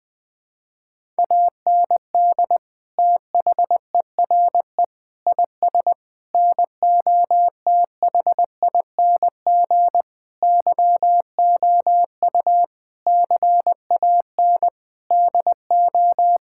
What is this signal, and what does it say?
Morse code, heard as a keyed tone, about 20 wpm, 700 Hz. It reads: AND THERE IS NOTHING YOU CAN DO
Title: Sentences with 8-Word Phrases - 20wpm